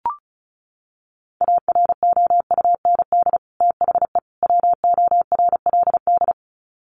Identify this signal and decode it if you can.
Morse code, heard as a keyed tone, about 34 wpm, 700 Hz. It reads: AROUND THE WORLD
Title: Sets of 3 Words 35wpm